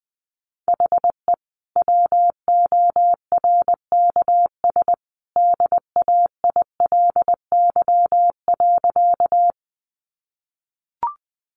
Morse code, heard as a keyed tone, about 20 wpm, 700 Hz. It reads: HE WORKS DAILY.
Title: Sentences from Top 1100 Words 20wpm